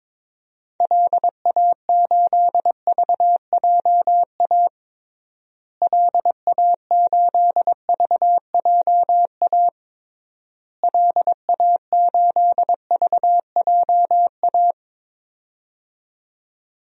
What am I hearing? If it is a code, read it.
Morse code, heard as a keyed tone, about 22 wpm, 700 Hz. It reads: LA8VJA LA8VJA LA8VJA
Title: DXCC-100 Call Signs - 6 Characters - Repeated 3x - 22wpm